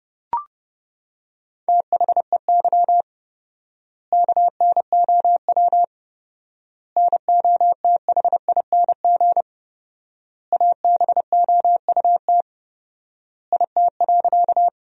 Morse code, heard as a keyed tone, about 30 wpm, 700 Hz. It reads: THEY KNOW NOTHING ABOUT IT.
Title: 4x Word-Spacing - Sentences from Top 500 Words - 30wpm